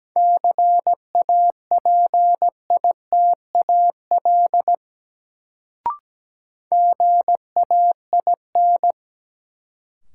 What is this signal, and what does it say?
Morse code, heard as a keyed tone, about 17 wpm, 700 Hz. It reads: CAPITAL GAIN
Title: Sentences from Top 1000 Words - New Word Drill - 17wpm